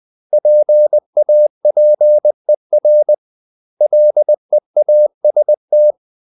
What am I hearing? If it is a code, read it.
Morse code, heard as a keyed tone, about 20 wpm, 600 Hz. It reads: PAPER LEAST
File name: CW / Practice / 500MostCommonEnglishWords20